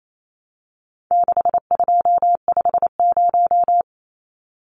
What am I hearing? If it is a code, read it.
Morse code, heard as a keyed tone, about 28 wpm, 700 Hz. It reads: 6250
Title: Numbers - 4-Digit 28wpm